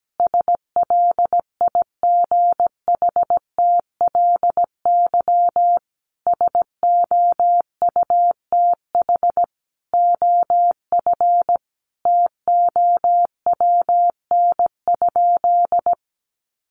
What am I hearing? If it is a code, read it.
Morse code, heard as a keyed tone, about 17 wpm, 700 Hz. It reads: SLIGHTLY SOUTH OF TOWN?
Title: Sentences from Top 1300 Words 17wpm